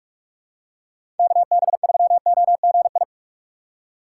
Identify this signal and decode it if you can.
Morse code, heard as a keyed tone, about 42 wpm, 700 Hz. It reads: KB3OGI
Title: USA Call Signs - 6 Characters - 4x Character Spacing - 45wpm